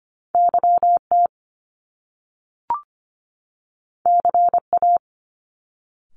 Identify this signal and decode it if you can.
Morse code, heard as a keyed tone, about 25 wpm, 700 Hz. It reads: YT CA
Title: Top Two Letter Combinations in Words 25wpm